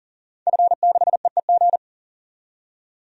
Morse code, heard as a keyed tone, about 40 wpm, 700 Hz. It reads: F6EEG
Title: DXCC-100 Call Signs - 5 Characters - 40wpm